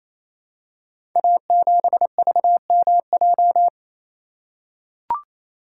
Morse code, heard as a keyed tone, about 28 wpm, 700 Hz. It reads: A7VMJ
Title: Code Groups - 5 Letter:Numbers - 28wpm